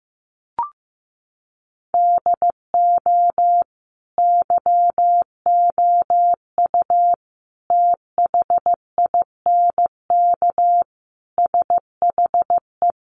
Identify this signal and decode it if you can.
Morse code, heard as a keyed tone, about 15 wpm, 700 Hz. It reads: DO YOU THINK SHE
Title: Sentences from Top 200 Words 15wpm